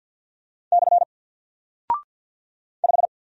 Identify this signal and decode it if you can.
Morse code, heard as a keyed tone, about 50 wpm, 700 Hz. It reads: / 5
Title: Single Character (Letter:Number:Punctuation:Prosign) 50wpm